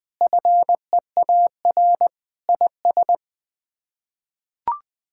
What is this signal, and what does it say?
Morse code, heard as a keyed tone, about 20 wpm, 700 Hz. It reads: FEAR IS
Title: Sets of 8 Words - 20wpm